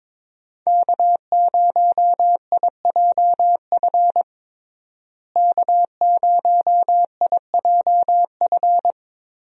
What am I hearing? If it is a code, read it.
Morse code, heard as a keyed tone, about 22 wpm, 700 Hz. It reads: K0IJF K0IJF
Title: USA Call Signs - 5 Characters - Repeated 3x - 22wpm